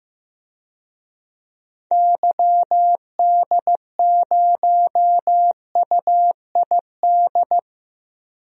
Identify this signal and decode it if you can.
Morse code, heard as a keyed tone, about 15 wpm, 700 Hz. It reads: YD0UID
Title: DXCC-100 Call Signs - 6 Characters - Repeated 3x - 15wpm